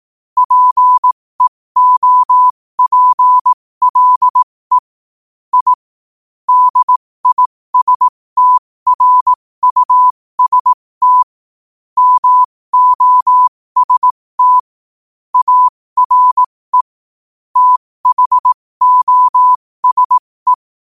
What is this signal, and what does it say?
Morse code, heard as a keyed tone, about 18 wpm, spaced as for 15 wpm, 1000 Hz. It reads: PEOPLE I DISTRUST MOST ARE THOSE
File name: qotdfiles / 15wpm